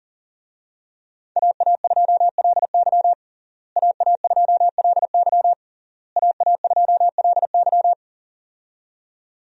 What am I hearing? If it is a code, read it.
Morse code, heard as a keyed tone, about 40 wpm, 700 Hz. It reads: AA2LY AA2LY AA2LY
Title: USA Call Signs - 5 Characters - Repeated 3x - 40wpm